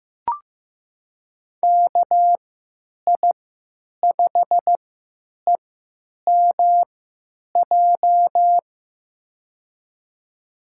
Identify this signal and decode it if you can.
Morse code, heard as a keyed tone, about 15 wpm, spaced as for 9 wpm, 700 Hz. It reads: KI5EMJ
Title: USA Call Signs - 6 Characters - 3x Character Spacing - 15wpm